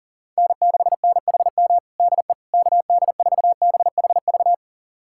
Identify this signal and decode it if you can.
Morse code, heard as a keyed tone, about 40 wpm, 700 Hz. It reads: N6NHM DE KD4BHV
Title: USA Callsign DE Callsign 40wpm